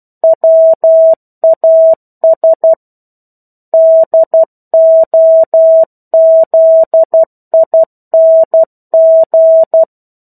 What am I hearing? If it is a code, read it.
Morse code, heard as a keyed tone, about 12 wpm, 650 Hz. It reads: WAS DOZING
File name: 01-Chapter-01 12 WPM